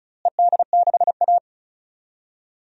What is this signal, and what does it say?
Morse code, heard as a keyed tone, about 34 wpm, 700 Hz. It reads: ED6A
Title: DXCC-100 Call Signs - 4 Characters - Repeated 2x - 35wpm